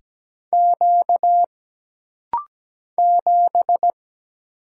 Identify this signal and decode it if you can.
Morse code, heard as a keyed tone, about 17 wpm, 700 Hz. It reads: Q 7
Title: Single Letter-Number - Rapid-Fire 17wpm